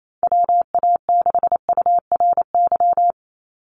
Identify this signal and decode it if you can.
Morse code, heard as a keyed tone, about 28 wpm, 700 Hz. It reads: WA6URY
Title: Call Signs - 28wpm